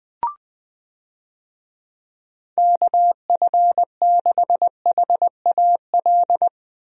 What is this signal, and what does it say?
Morse code, heard as a keyed tone, about 21 wpm, 700 Hz. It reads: KF6HAL
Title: USA Call Signs - 6 Characters - 20wpm